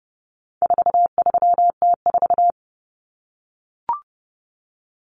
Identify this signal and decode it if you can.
Morse code, heard as a keyed tone, about 30 wpm, 700 Hz. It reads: 43T4
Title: Common Cut Numbers - 4-Digit 30wpm